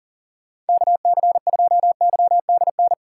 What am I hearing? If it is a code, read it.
Morse code, heard as a keyed tone, about 49 wpm, 700 Hz. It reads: KC2YDN
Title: Speed Racing - USA 2x3 Call Signs 40wpm